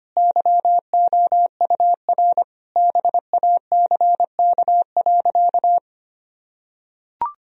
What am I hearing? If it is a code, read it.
Morse code, heard as a keyed tone, about 25 wpm, 700 Hz. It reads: YOUR BACK.
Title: Sentences from Top 1000 Words - Review Part 1 - 25wpm